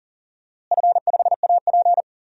Morse code, heard as a keyed tone, about 40 wpm, 700 Hz. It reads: F5AP
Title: DXCC-100 Call Signs - 4 Characters - Repeated 3x - 40wpm